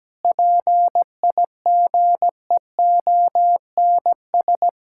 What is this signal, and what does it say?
Morse code, heard as a keyed tone, about 17 wpm, 700 Hz. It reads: PIGEONS
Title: Pangram Challenge v2 - 17wpm